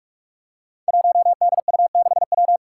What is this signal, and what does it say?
Morse code, heard as a keyed tone, about 46 wpm, 700 Hz. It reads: 1DU6W
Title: Five Challenging Letter-Number Combinations 45wpm